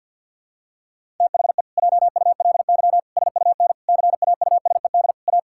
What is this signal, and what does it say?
Morse code, heard as a keyed tone, about 51 wpm, 700 Hz. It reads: THE JULY SUN CAUSED A
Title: Pangram Challenge v2 - 50wpm